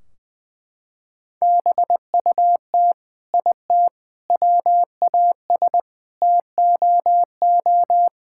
Morse code, heard as a keyed tone, about 20 wpm, 700 Hz. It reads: BUT IT WAS TOO
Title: Sentences from Top 1600 Words 20wpm